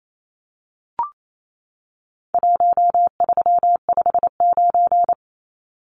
Morse code, heard as a keyed tone, about 28 wpm, 700 Hz. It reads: 1359
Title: Numbers - Contest Style - 4-Digit 28wpm